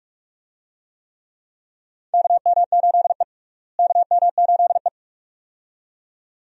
Morse code, heard as a keyed tone, about 46 wpm, 700 Hz. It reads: KM8E KM8E
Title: USA Call Signs - 4 Characters - Repeated 2x - 45wpm